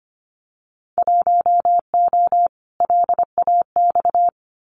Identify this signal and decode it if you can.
Morse code, heard as a keyed tone, about 25 wpm, 700 Hz. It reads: 1O LAX
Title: Winter Field Day Exchanges 25wpm